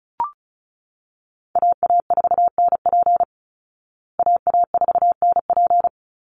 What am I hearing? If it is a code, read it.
Morse code, heard as a keyed tone, about 34 wpm, 700 Hz. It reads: AA4NP AA4NP
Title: Call Signs Repeated 2x - 35wpm